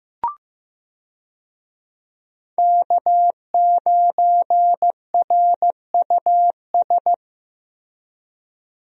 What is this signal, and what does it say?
Morse code, heard as a keyed tone, about 15 wpm, 700 Hz. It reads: K9RUS
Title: USA Call Signs - 5 Characters - Repeated 3x - 15wpm